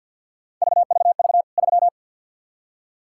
Morse code, heard as a keyed tone, about 49 wpm, 700 Hz. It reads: VVV 3
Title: K1USN SST QSO Practice 50wpm